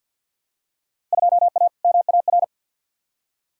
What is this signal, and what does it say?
Morse code, heard as a keyed tone, about 54 wpm, 700 Hz. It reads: 2A MAR
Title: Field Day Exchanges - 50wpm